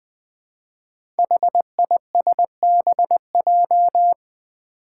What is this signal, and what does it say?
Morse code, heard as a keyed tone, about 20 wpm, 700 Hz. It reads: HISBJ
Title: Five Challenging Letter-Number Combinations 20wpm